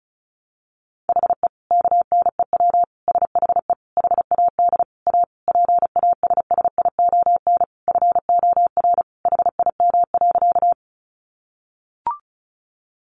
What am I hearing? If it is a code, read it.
Morse code, heard as a keyed tone, about 34 wpm, 700 Hz. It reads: HE KNEW SHE HAD A PASSION FOR HIM.
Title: Sentences from Top 1300 Words 35wpm